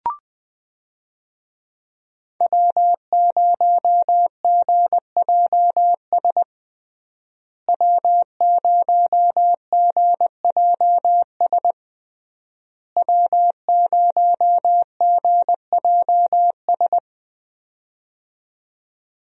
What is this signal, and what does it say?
Morse code, heard as a keyed tone, about 20 wpm, 700 Hz. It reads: W0GJS W0GJS W0GJS
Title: USA Call Signs - 5 Characters - Repeated 3x - 20wpm